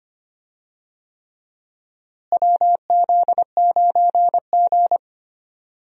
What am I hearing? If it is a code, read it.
Morse code, heard as a keyed tone, about 25 wpm, 700 Hz. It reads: WZ9G
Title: USA Call Signs - 4 Characters - Repeated 3x - 25wpm